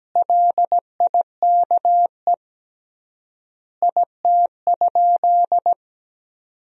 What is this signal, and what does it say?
Morse code, heard as a keyed tone, about 17 wpm, 700 Hz. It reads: LIKE IT?
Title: 3x Word-Spacing - Sentences from Top 300 Words - 17wpm